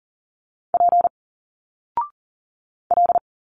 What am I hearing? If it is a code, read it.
Morse code, heard as a keyed tone, about 40 wpm, 700 Hz. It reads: P L
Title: Single Letters 40wpm